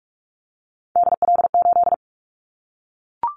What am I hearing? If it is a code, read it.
Morse code, heard as a keyed tone, about 44 wpm, 700 Hz. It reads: DL8
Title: Callsign Prefixes for Top 25 DXCC - Repeated 3x - 45wpm